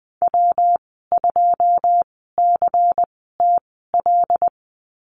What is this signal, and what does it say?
Morse code, heard as a keyed tone, about 19 wpm, spaced as for 14 wpm, 700 Hz. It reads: W2CTL
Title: 122 Lesson 20g - Number 3 - Call Signs 20wpm